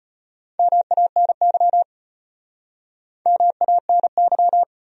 Morse code, heard as a keyed tone, about 35 wpm, 700 Hz. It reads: MANY MANY